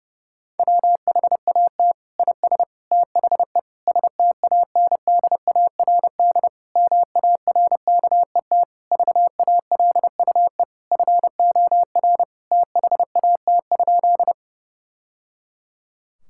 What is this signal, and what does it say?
Morse code, heard as a keyed tone, about 32 wpm, 700 Hz. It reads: WHAT IS THE STANDARD MARKET VALUE FOR THAT?
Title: Sentences from Top 1000 Words - Review Part 6 - 30wpm